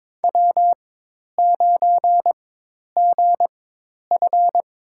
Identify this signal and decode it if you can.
Morse code, heard as a keyed tone, about 22 wpm, spaced as for 10 wpm, 700 Hz. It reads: W9GF
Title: USA Call Signs - 4x Character Spacing - 22wpm